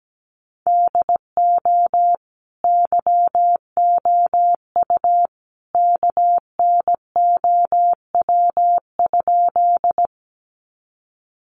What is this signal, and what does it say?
Morse code, heard as a keyed tone, about 17 wpm, 700 Hz. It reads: DO YOU KNOW?
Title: Sentences from Top 100 Words 17wpm